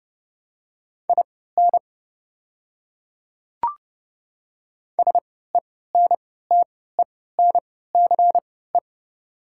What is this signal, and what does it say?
Morse code, heard as a keyed tone, about 30 wpm, spaced as for 17 wpm, 700 Hz. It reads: IN SENTENCE
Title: 3x Character Spacing - Top 200 Words - 30wpm